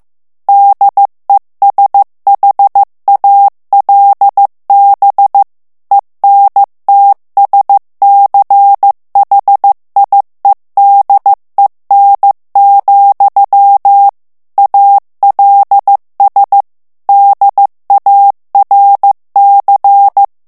Morse code, heard as a keyed tone, about 15 wpm, 800 Hz. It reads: DESHALB ENTSCHIEDEN, ALS DARC
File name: morse mp3 / ALLEMAND texte clair 1 vitesse 14 mots minute